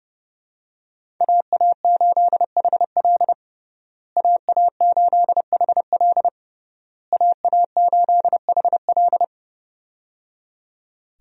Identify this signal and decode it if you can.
Morse code, heard as a keyed tone, about 30 wpm, 700 Hz. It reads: AA8HL AA8HL AA8HL
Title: USA Call Signs - 5 Characters - Repeated 3x - 30wpm